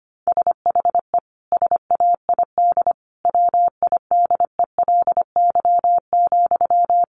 Morse code, heard as a keyed tone, about 25 wpm, 700 Hz. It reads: SHE SAID WIDELY,
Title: Sentences from Top 1200 Words 25wpm